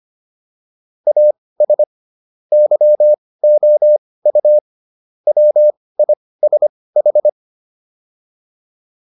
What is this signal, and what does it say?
Morse code, heard as a keyed tone, about 25 wpm, spaced as for 18 wpm, 600 Hz. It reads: AS YOU WISH